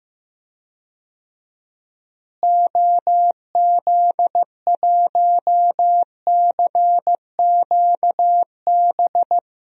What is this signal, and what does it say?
Morse code, heard as a keyed tone, about 15 wpm, 700 Hz. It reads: OZ1CQB
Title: DXCC-100 Call Signs - 6 Characters - Repeated 3x - 15wpm